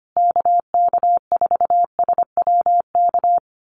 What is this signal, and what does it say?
Morse code, heard as a keyed tone, about 25 wpm, 700 Hz. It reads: KK4SWK
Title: USA Call Signs Repeated 2x 25wpm